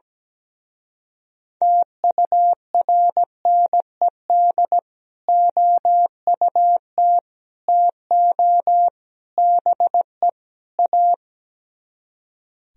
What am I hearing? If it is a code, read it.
Morse code, heard as a keyed tone, about 17 wpm, 700 Hz. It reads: TURNED OUT TO BE A
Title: Speed-Racing - Sets of 5 Words 17wpm